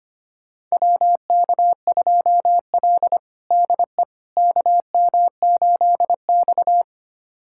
Morse code, heard as a keyed tone, about 25 wpm, 700 Hz. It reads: WK2L DE KM8X
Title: USA Callsign DE Callsign - 4 Characters - 25wpm